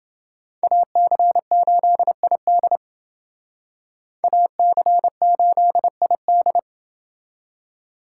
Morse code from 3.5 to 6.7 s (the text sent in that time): AC8ID